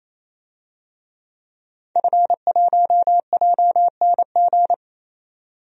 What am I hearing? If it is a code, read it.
Morse code, heard as a keyed tone, about 28 wpm, 700 Hz. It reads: F1JNG
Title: DXCC-100 Call Signs - 5 Characters - 28wpm